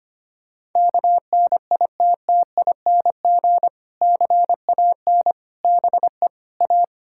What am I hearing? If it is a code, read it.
Morse code, heard as a keyed tone, about 25 wpm, 700 Hz. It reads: KNITTING CAN BE A
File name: practice / 25wpm / Difficult Sentences with Hints - 25wpm